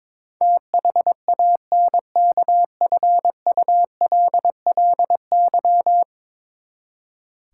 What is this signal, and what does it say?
Morse code, heard as a keyed tone, about 21 wpm, 700 Hz. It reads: THANKFULLY
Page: morse-code-ninja-practice-22wpm